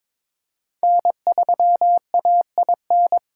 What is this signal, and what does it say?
Morse code, heard as a keyed tone, about 22 wpm, 700 Hz. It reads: N3AIN
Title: USA Callsign DE Callsign 22wpm